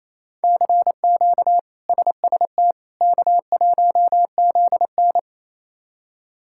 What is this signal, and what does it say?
Morse code, heard as a keyed tone, about 28 wpm, 700 Hz. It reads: CQ SST K1ZN
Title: K1USN SST QSO Practice - 28wpm